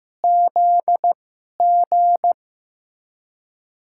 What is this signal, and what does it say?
Morse code, heard as a keyed tone, about 15 wpm, spaced as for 11 wpm, 700 Hz. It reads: ZG
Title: USA Call Signs - 2x Character Spacing - 15wpm